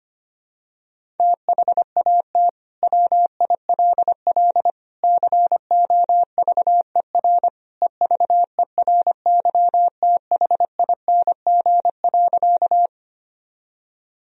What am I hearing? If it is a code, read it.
Morse code, heard as a keyed tone, about 25 wpm, 700 Hz. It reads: THAT WILL COVER EVERYTHING.
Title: Sentences from Top 1000 Words - Review Part 1 - 25wpm